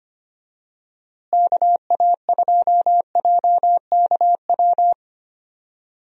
Morse code, heard as a keyed tone, about 25 wpm, 700 Hz. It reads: KA2JKW
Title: USA Call Signs - 6 Characters - Repeated 3x - 25wpm